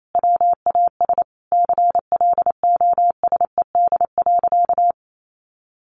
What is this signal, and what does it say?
Morse code, heard as a keyed tone, about 28 wpm, 700 Hz. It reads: WAS CLOSED.
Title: Sentences from Top 1400 Words 28wpm